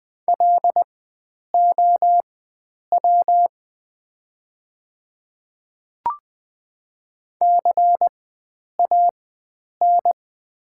Morse code, heard as a keyed tone, about 20 wpm, spaced as for 9 wpm, 700 Hz. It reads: LOW CAN
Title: 4x Character Spacing - Top 300 Words - 20wpm